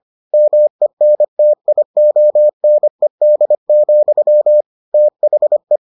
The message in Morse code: MENTIONED, THE

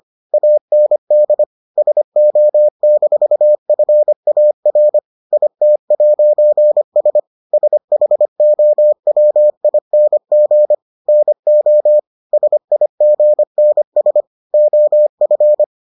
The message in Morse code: AND SO-FAR IT'S SHOWING NO SIGNS OF

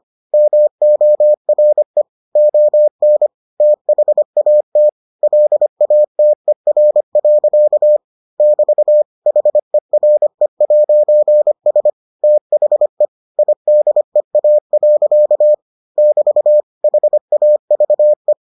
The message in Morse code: MORE ON THAT LATER. <BT> HERE'S THE IDEA. <BT> HAVE

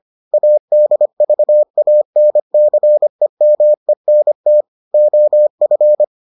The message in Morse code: ADVANCEMENT OF

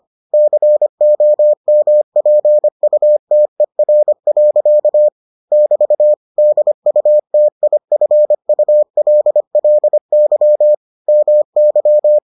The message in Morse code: COMPUTER. <BT> DUTIFULLY MY

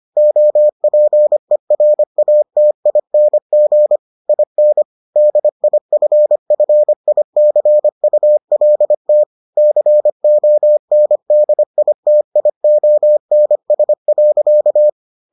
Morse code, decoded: OPERATING IN DIFFICULT CONDITIONS.

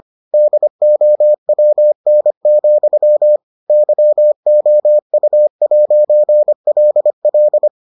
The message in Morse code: DOWN, YOU'LL

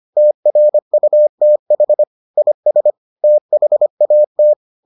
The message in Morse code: TRUTH IS THAT

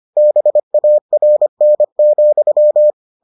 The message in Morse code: BARN,

